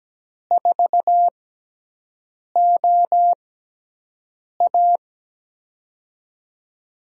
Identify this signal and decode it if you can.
Morse code, heard as a keyed tone, about 17 wpm, spaced as for 6 wpm, 700 Hz. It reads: 4OA
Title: USA Call Signs - 6x Character Spacing - 17wpm